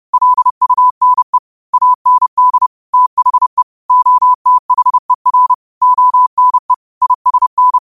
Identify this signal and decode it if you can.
Morse code, heard as a keyed tone, about 30 wpm, 1000 Hz. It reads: LUNE AND THE OTHER ONE ISN
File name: qotdfiles / 30wpm